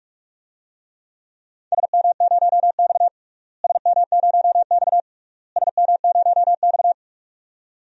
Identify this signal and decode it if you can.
Morse code, heard as a keyed tone, about 42 wpm, 700 Hz. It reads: SM0X SM0X SM0X
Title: DXCC-100 Call Signs - 4 Characters - Repeated 3x - 45wpm